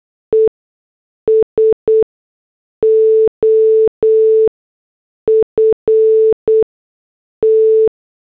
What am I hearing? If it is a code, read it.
Morse code, heard as a keyed tone, about 8 wpm, spaced as for 6 wpm, 440 Hz. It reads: E SOFT